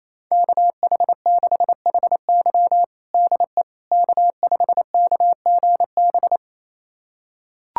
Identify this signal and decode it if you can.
Morse code, heard as a keyed tone, about 28 wpm, 700 Hz. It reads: KH6HY DE K5KGB
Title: USA Callsign DE Callsign - 5 Characters - 28wpm